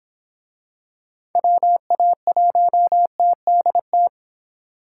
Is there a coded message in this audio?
Yes: Morse code from 1.4 to 4.1 s, heard as a keyed tone, about 23 wpm, 700 Hz. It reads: WA1TDT